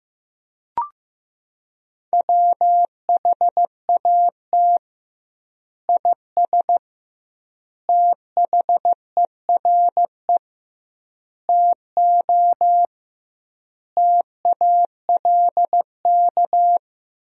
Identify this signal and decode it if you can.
Morse code, heard as a keyed tone, about 15 wpm, 700 Hz. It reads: WHAT IS THERE TO TALK
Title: 2x Word-Spacing - Sentences from Top 400 Words - 15wpm